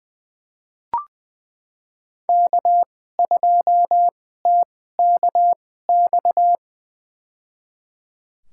Morse code, heard as a keyed tone, about 20 wpm, spaced as for 14 wpm, 700 Hz. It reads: K2TKX
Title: USA Call Signs - 2x Character Spacing - 20wpm